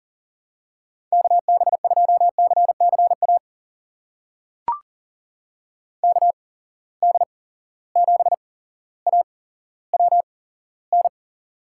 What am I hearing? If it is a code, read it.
Morse code, heard as a keyed tone, about 40 wpm, spaced as for 17 wpm, 700 Hz. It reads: KB2CCA KD7AWN
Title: USA Call Signs - 8x Character Spacing - 40wpm